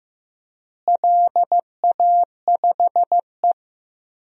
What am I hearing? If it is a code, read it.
Morse code, heard as a keyed tone, about 15 wpm, 700 Hz. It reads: LA5E
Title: DXCC-100 Call Signs - 4 Characters - Repeated 3x - 15wpm